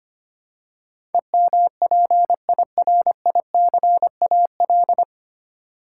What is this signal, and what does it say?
Morse code, heard as a keyed tone, about 25 wpm, 690 Hz. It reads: EMPIRICAL
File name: practice / 25wpm / 9 Letter Words - First 3 Letters Given - 25wpm